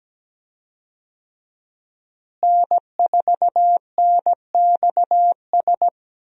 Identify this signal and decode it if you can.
Morse code, heard as a keyed tone, about 17 wpm, 700 Hz. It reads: N4NXS
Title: USA Call Signs - 5 Characters - Repeated 2x - 17wpm